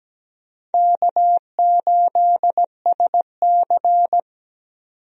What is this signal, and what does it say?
Morse code, heard as a keyed tone, about 17 wpm, 700 Hz. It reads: K8SC
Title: USA Call Signs - 4 Characters - 17wpm